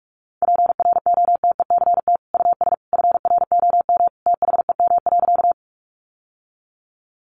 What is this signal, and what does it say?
Morse code, heard as a keyed tone, about 44 wpm, 700 Hz. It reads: PROTECT US FROM THEM.
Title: Sentences from Top 900 Words 45wpm